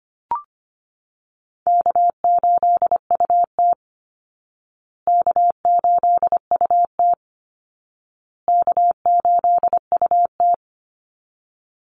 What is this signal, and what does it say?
Morse code, heard as a keyed tone, about 25 wpm, 700 Hz. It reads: K8UT K8UT K8UT
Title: Easy Call Signs Repeated 3x - 25wpm